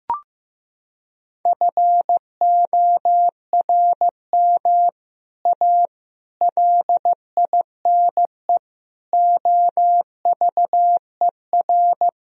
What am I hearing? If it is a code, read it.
Morse code, heard as a keyed tone, about 16 wpm, 700 Hz. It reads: FORM A LINE OVER
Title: Sentences from Top 1000 Words - Review Part 1 - 15wpm